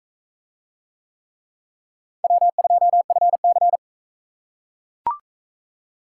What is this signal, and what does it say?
Morse code, heard as a keyed tone, about 40 wpm, 700 Hz. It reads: W2FC